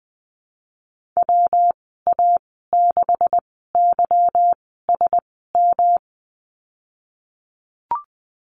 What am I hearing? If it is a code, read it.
Morse code, heard as a keyed tone, about 20 wpm, spaced as for 15 wpm, 700 Hz. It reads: WA6YSM
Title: 167 Lesson 27g - Number 7 - Call Signs 20wpm